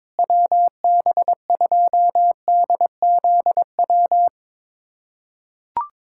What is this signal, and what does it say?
Morse code, heard as a keyed tone, about 21 wpm, 700 Hz. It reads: WB2DZW
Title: USA Callsign DE Callsign - 6 Characters - 22wpm